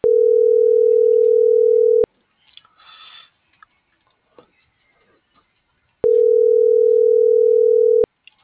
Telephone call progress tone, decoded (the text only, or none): ringback tone